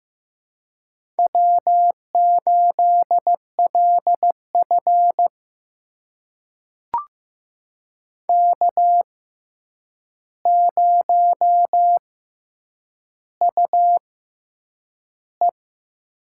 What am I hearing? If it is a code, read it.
Morse code, heard as a keyed tone, about 15 wpm, spaced as for 8 wpm, 700 Hz. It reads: W8LF K0UE
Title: USA Call Signs - 4 Characters - 6x Character Spacing - 15wpm